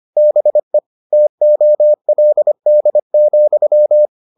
Morse code, heard as a keyed tone, about 25 wpm, 600 Hz. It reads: BE TOLD,